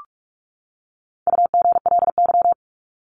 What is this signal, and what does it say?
Morse code, heard as a keyed tone, about 44 wpm, 700 Hz. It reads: UGLY
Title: Top 1751-2000 Words 45wpm